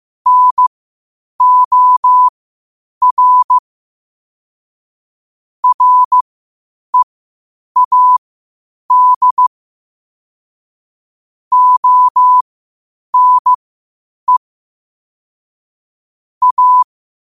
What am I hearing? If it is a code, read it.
Morse code, heard as a keyed tone, about 15 wpm, spaced as for 8 wpm, 1000 Hz. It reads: NOR READ ONE A
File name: qotdfiles / 8wpm